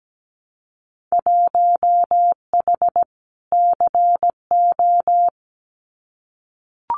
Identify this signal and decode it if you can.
Morse code, heard as a keyed tone, about 17 wpm, 700 Hz. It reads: 1H CO
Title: Winter Field Day Exchanges 17wpm